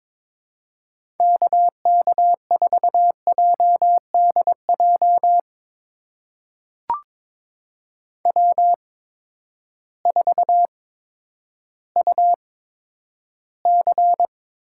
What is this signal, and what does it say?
Morse code, heard as a keyed tone, about 21 wpm, spaced as for 11 wpm, 700 Hz. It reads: KK4JDJ W4UC
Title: USA Call Signs - 8x Character Spacing - 22wpm